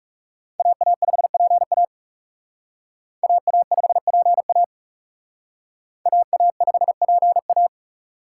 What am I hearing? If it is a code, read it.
Morse code, heard as a keyed tone, about 37 wpm, 700 Hz. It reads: AA5PA AA5PA AA5PA
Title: Speed Racing - USA 2x2 Call Signs 30wpm